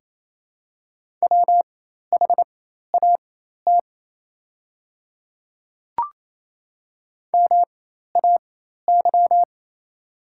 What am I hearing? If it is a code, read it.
Morse code, heard as a keyed tone, about 28 wpm, spaced as for 13 wpm, 700 Hz. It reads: WHAT MAY
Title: 4x Character Spacing - Top 200 Words - 28wpm